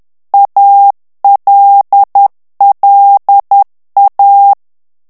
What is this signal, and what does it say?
Morse code, heard as a keyed tone, about 10 wpm, 800 Hz. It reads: ALLA
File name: ITALIEN texte clair 2 vitesse 10 mots minute